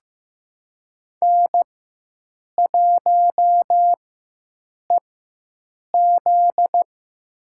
Morse code, heard as a keyed tone, about 15 wpm, spaced as for 7 wpm, 700 Hz. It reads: N1EZ
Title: USA Call Signs - 4x Character Spacing - 15wpm